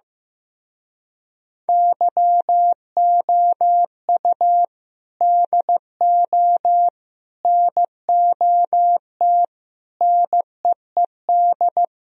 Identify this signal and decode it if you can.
Morse code, heard as a keyed tone, about 15 wpm, 700 Hz. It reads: YOU DO NOT NEED
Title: Speed-Racing - Sets of 5 Words 15wpm